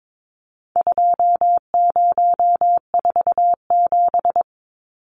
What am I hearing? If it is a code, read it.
Morse code, heard as a keyed tone, about 22 wpm, 700 Hz. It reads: 2047
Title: Numbers - 4-Digit 22wpm